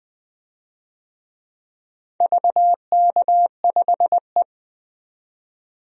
Morse code, heard as a keyed tone, about 20 wpm, 700 Hz. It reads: VK5E
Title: DXCC-100 Call Signs - 4 Characters - 20wpm